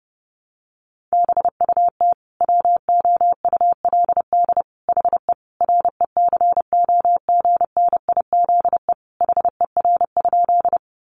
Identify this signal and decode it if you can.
Morse code, heard as a keyed tone, about 30 wpm, 700 Hz. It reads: BUT WOULD HE RECOGNIZE HER?
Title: Sentences from Top 1400 Words 30wpm